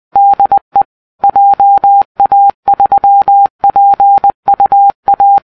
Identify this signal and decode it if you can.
Morse code, heard as a keyed tone, about 20 wpm, 800 Hz. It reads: DE JA3PUA